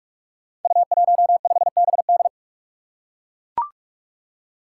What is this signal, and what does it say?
Morse code, heard as a keyed tone, about 42 wpm, 700 Hz. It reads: U15BD
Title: Five Challenging Letter-Number Combinations 45wpm